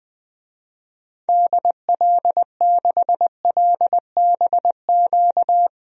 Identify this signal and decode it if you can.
Morse code, heard as a keyed tone, about 21 wpm, 700 Hz. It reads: DL6LBQ
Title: DXCC-100 Call Signs - 6 Characters - Repeated 3x - 20wpm